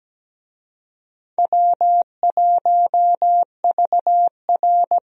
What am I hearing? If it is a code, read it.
Morse code, heard as a keyed tone, about 17 wpm, 700 Hz. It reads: W1VR
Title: USA Call Signs - 4 Characters - Repeated 2x - 17wpm